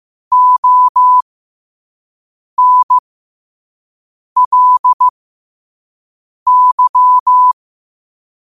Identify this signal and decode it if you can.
Morse code, heard as a keyed tone, about 15 wpm, spaced as for 5 wpm, 1000 Hz. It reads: ONLY
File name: qotdfiles / 5wpm